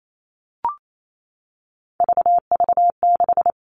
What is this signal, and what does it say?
Morse code, heard as a keyed tone, about 28 wpm, 700 Hz. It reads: VV6
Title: All Cut Numbers - 3-Digit 28wpm